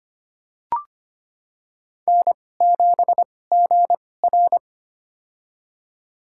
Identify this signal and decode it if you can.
Morse code, heard as a keyed tone, about 25 wpm, spaced as for 18 wpm, 700 Hz. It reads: N7GR
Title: USA Call Signs - 4 Characters - 2x Character Spacing - 25wpm